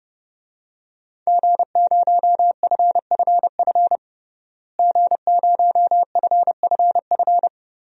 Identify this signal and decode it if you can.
Morse code, heard as a keyed tone, about 30 wpm, 700 Hz. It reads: G0FFF G0FFF
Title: DXCC-100 Call Signs - 5 Characters - Repeated 3x - 30wpm